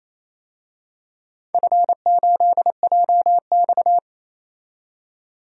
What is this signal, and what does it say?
Morse code, heard as a keyed tone, about 28 wpm, 700 Hz. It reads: F8JX
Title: DXCC-100 Call Signs - 4 Characters - 28wpm